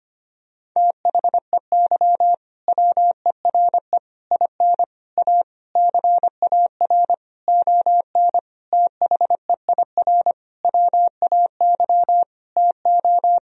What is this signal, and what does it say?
Morse code, heard as a keyed tone, about 25 wpm, 700 Hz. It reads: THEY WERE IN A CAR ON THEIR WAY TO
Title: Sentences from Top 1000 Words - Review Part 1 - 25wpm